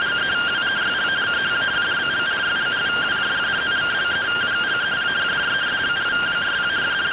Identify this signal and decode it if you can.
Two-tone teleprinter signal, about 170 Hz shift.